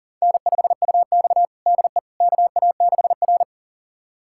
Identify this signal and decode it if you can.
Morse code, heard as a keyed tone, about 40 wpm, 700 Hz. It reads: N5UX DE KA6R
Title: USA Callsign DE Callsign - 4 Characters - 40wpm